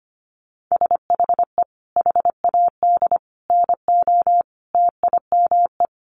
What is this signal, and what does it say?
Morse code, heard as a keyed tone, about 25 wpm, 700 Hz. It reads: SHE HAD NO TIME
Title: Sentences from Top 1100 Words 25wpm